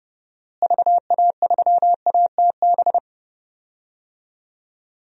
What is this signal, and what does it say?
Morse code, heard as a keyed tone, about 28 wpm, 700 Hz. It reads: VA3ATB
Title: DXCC-100 Call Signs - 6 Characters - Repeated 3x - 30wpm